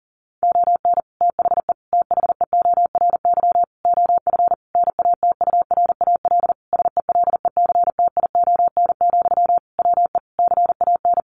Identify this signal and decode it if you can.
Morse code, heard as a keyed tone, about 40 wpm, 700 Hz. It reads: ON THE THEORY OF NATURAL SELECTION, WE CAN
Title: Sentences from Top 1400 Words 40wpm